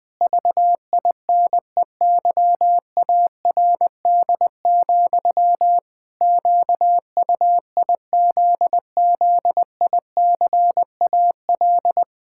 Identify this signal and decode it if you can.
Morse code, heard as a keyed tone, about 21 wpm, 700 Hz. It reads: VINEYARD, QUIZZICAL
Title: Pangram Challenge v2 - 20wpm